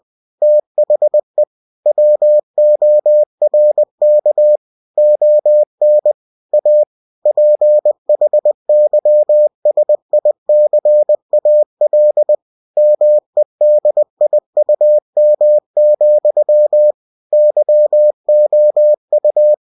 Morse code, heard as a keyed tone, about 20 wpm, 600 Hz. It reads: THE WORK ON A PHYSICAL MEDIUM, YOU
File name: cw-books / winnie-the-pooh / winnie-20wpm-page-0119